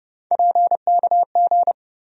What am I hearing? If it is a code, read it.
Morse code, heard as a keyed tone, about 30 wpm, 700 Hz. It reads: PKG
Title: USA Call Sign Postfixes - 3 Letters 30wpm